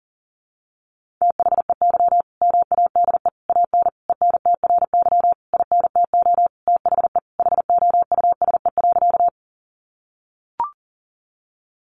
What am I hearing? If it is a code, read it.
Morse code, heard as a keyed tone, about 40 wpm, 700 Hz. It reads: THEY MADE AN ENTRY INTO THE HOUSE.
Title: Sentences from Top 1500 Words 40wpm